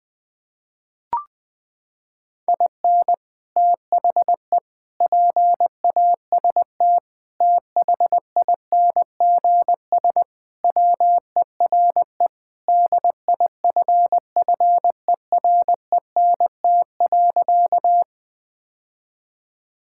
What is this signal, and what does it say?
Morse code, heard as a keyed tone, about 20 wpm, 700 Hz. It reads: IN THE PAST THINGS WERE DIFFERENT.
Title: Sentences with 3-Word Phrases - 20wpm